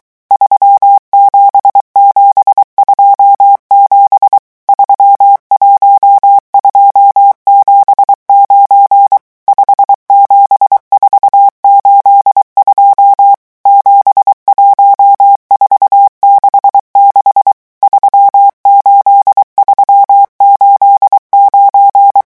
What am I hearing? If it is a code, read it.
Morse code, heard as a keyed tone, about 23 wpm, 800 Hz. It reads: 37727 31279 57482 71466 38389
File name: morse mp3 / revision11 chiffres vitesse 22